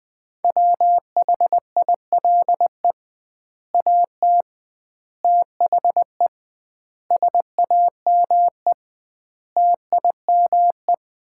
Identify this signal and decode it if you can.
Morse code, heard as a keyed tone, about 21 wpm, 700 Hz. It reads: WHILE AT THE SAME TIME
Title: 2x Word-Spacing - Sets of 5 Words - 20wpm